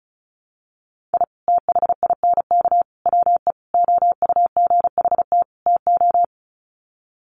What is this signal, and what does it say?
Morse code, heard as a keyed tone, about 34 wpm, 700 Hz. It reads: I THINK WE OUGHT TO
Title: Sets of 5 Words Encore 35wpm